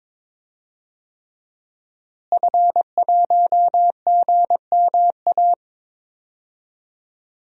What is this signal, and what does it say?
Morse code, heard as a keyed tone, about 21 wpm, 700 Hz. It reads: F1GMA
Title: DXCC-100 Call Signs - 5 Characters - Repeated 3x - 22wpm